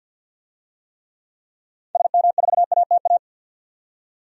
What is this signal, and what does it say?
Morse code, heard as a keyed tone, about 51 wpm, 700 Hz. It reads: SM4ATA
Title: DXCC-100 Call Signs - 6 Characters - 50wpm